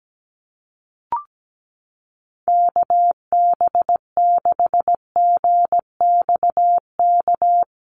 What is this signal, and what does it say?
Morse code, heard as a keyed tone, about 17 wpm, 700 Hz. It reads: KB6GXK
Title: USA Call Signs 17wpm